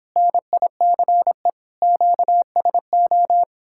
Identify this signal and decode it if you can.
Morse code, heard as a keyed tone, about 29 wpm, 700 Hz. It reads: NICE QSO